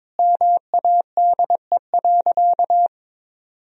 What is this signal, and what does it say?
Morse code, heard as a keyed tone, about 22 wpm, 700 Hz. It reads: MADE.
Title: Sentences from Top 1000 Words - Review Part 4 - 22wpm